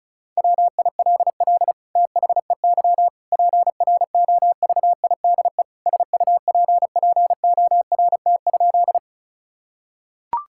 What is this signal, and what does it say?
Morse code, heard as a keyed tone, about 36 wpm, 700 Hz. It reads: WILL THEY PROVIDE SUPPORT?
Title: Sentences from Top 1000 Words - Review Part 5 - 35wpm